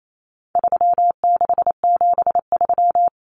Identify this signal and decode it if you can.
Morse code, heard as a keyed tone, about 28 wpm, 700 Hz. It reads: 3673
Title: Common Cut Numbers - 4-Digit 28wpm